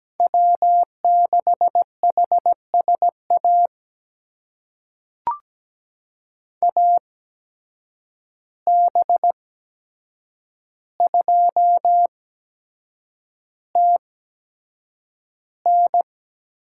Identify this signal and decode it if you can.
Morse code, heard as a keyed tone, about 17 wpm, spaced as for 7 wpm, 700 Hz. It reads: W6HSA AB2TN
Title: USA Call Signs - 8x Character Spacing - 17wpm